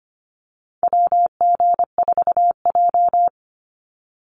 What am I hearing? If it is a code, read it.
Morse code, heard as a keyed tone, about 25 wpm, 700 Hz. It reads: WG4J
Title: USA Call Signs 25wpm